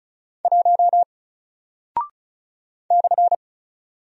Morse code, heard as a keyed tone, about 36 wpm, 700 Hz. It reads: 1 /
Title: Single Character (Letter:Number:Punctuation:Prosign) Alt 35wpm